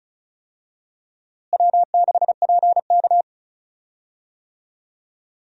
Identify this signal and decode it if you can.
Morse code, heard as a keyed tone, about 34 wpm, 700 Hz. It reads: W6PK
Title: USA Call Signs - 4 Characters - Repeated 2x - 35wpm